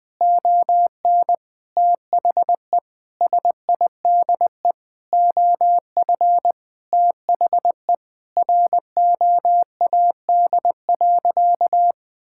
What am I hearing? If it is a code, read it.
Morse code, heard as a keyed tone, about 20 wpm, 700 Hz. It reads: ON THE SIDE OF THE ROAD.
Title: Sentences with 5-Word Phrases - 20wpm